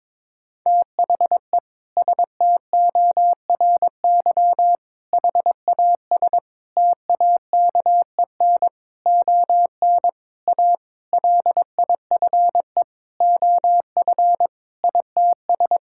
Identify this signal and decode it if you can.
Morse code, heard as a keyed tone, about 22 wpm, 700 Hz. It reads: THE STORY HAS TAKEN ON A LIFE OF ITS
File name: Sentences with 7-Word Phrases - 22wpm